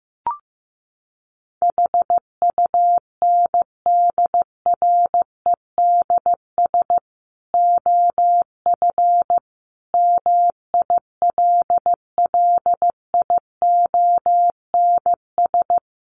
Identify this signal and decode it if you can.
Morse code, heard as a keyed tone, about 15 wpm, 700 Hz. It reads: HUNDREDS OF MILLIONS
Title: Sets of 5 Words 15wpm